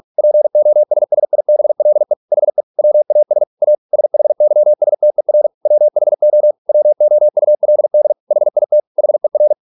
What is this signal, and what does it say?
Morse code, heard as a keyed tone, about 46 wpm, 600 Hz. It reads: POSSIBLE HE WAS A SHYSTER WHO WOULD HIT HER